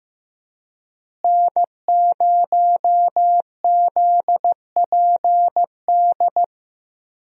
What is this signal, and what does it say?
Morse code, heard as a keyed tone, about 15 wpm, 700 Hz. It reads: N0ZPD
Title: USA Call Signs - 5 Characters - Repeated 3x - 15wpm